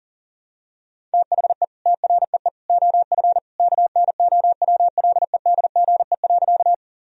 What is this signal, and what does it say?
Morse code, heard as a keyed tone, about 37 wpm, 700 Hz. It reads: THE TREE OF KNOWLEDGE.
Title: Sentences from Top 1000 Words - Review Part 3 - 40wpm